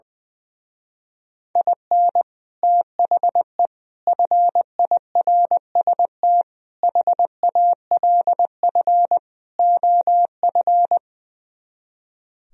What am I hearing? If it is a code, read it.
Morse code, heard as a keyed tone, about 21 wpm, 700 Hz. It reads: IN THE FIRST HALF OF
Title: Speed-Racing - Sets of 5 Words 20wpm